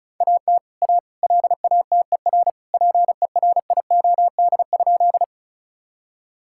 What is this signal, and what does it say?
Morse code, heard as a keyed tone, about 36 wpm, 700 Hz. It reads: AT A LATER PERIOD?
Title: Sentences from Top 1000 Words - Review Part 5 - 35wpm